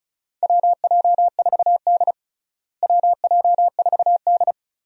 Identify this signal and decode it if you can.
Morse code, heard as a keyed tone, about 34 wpm, 700 Hz. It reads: WJ4D WJ4D
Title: USA Call Signs - 4 Characters - Repeated 3x - 35wpm